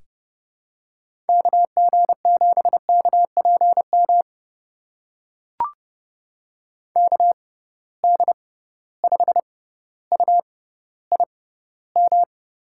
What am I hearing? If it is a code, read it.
Morse code, heard as a keyed tone, about 30 wpm, spaced as for 15 wpm, 700 Hz. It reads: KG7KPM KD5UIM
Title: USA Call Signs - 6 Characters - 6x Character Spacing - 30wpm